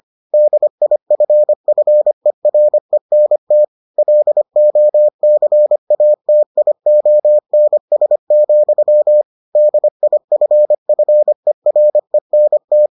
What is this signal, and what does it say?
Morse code, heard as a keyed tone, about 25 wpm, 600 Hz. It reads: DIFFERENT LOCATIONS, DIFFERENT